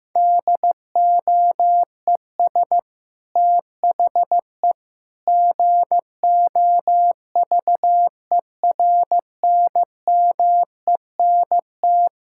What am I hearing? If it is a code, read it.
Morse code, heard as a keyed tone, about 15 wpm, 700 Hz. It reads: DOES THE GOVERNMENT
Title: Sentences from Top 1600 Words 15wpm